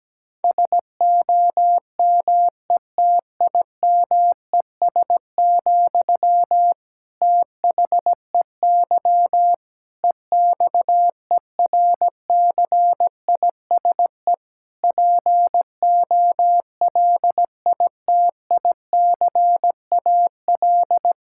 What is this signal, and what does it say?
Morse code, heard as a keyed tone, about 17 wpm, 700 Hz. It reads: SOMETIMES, THEY EXERCISE POLITICAL
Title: Sentences from Top 1000 Words - Review Part 6 - 17wpm